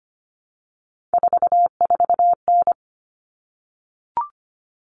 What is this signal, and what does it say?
Morse code, heard as a keyed tone, about 25 wpm, 700 Hz. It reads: 44N
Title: Common Cut Numbers - 3-Digit 25wpm